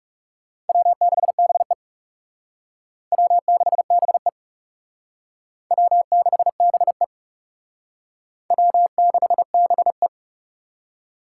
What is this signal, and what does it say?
Morse code, heard as a keyed tone, about 37 wpm, 700 Hz. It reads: W6BE W6BE W6BE W6BE
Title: Speed Racing - USA 1x2 Call Signs 30wpm